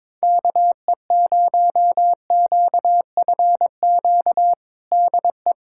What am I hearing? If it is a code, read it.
Morse code, heard as a keyed tone, about 22 wpm, 700 Hz. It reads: KE0QFQ DE
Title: USA Callsign DE Callsign 22wpm